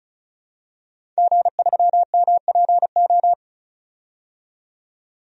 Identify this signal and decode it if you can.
Morse code, heard as a keyed tone, about 36 wpm, 700 Hz. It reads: G3MPO
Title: DXCC-100 Call Signs - 5 Characters - 35wpm